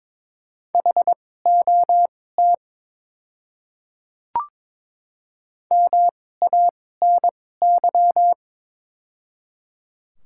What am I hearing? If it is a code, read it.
Morse code, heard as a keyed tone, about 22 wpm, spaced as for 16 wpm, 700 Hz. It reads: HOT MANY
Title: 2x Character Spacing - Top 100 Words - 22wpm